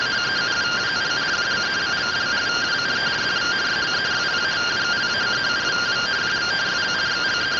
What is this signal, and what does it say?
Two-tone teleprinter signal, about 170 Hz shift.